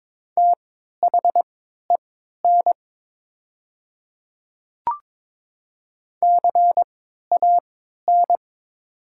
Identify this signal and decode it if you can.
Morse code, heard as a keyed tone, about 22 wpm, spaced as for 13 wpm, 700 Hz. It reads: THEN CAN
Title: 3x Character Spacing - Top 100 Words - 22wpm